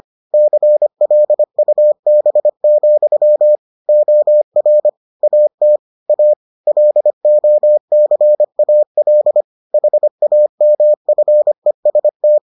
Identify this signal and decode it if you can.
Morse code, heard as a keyed tone, about 25 wpm, 600 Hz. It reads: CLUB, OR AT A LOCAL HAMFEST